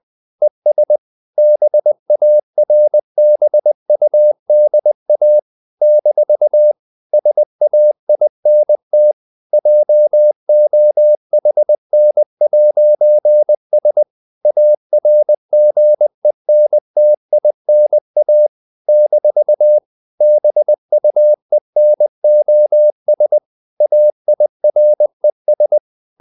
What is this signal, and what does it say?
Morse code, heard as a keyed tone, about 20 wpm, 600 Hz. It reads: ES BARBUDA - SAINT JOHN'S ARGENTINA - BUENOS AIRES
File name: CW / Practice / NationalCapitals20